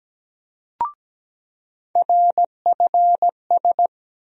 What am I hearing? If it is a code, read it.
Morse code, heard as a keyed tone, about 17 wpm, 700 Hz. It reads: RFS
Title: USA Call Sign Postfixes - 3 Letters 17wpm